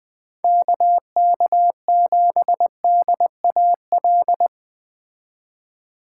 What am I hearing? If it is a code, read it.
Morse code, heard as a keyed tone, about 20 wpm, 700 Hz. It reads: KK7DAL